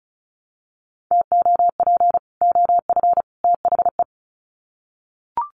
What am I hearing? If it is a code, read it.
Morse code, heard as a keyed tone, about 34 wpm, 700 Hz. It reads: TOP OF THE
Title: Sets of 3 Words 35wpm